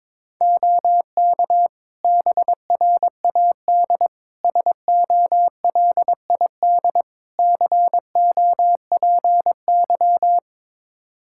Morse code, heard as a keyed tone, about 22 wpm, 700 Hz. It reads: OK BRAD SOLID COPY
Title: K1USN SST QSO Practice 22wpm